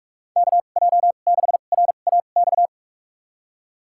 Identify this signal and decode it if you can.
Morse code, heard as a keyed tone, about 46 wpm, spaced as for 32 wpm, 700 Hz. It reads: KJ6RAX
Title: USA Call Signs - 2x Character Spacing - 45wpm